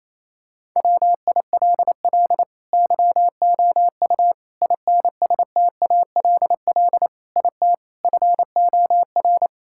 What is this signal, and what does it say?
Morse code, heard as a keyed tone, about 28 wpm, 700 Hz. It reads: WILL YOU INSTALL IT FOR
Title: Sentences from Top 1600 Words 28wpm